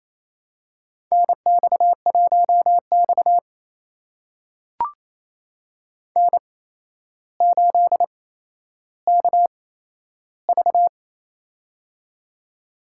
Morse code, heard as a keyed tone, about 28 wpm, spaced as for 12 wpm, 700 Hz. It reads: NX1X N8KV
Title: USA Call Signs - 4 Characters - 8x Character Spacing - 28wpm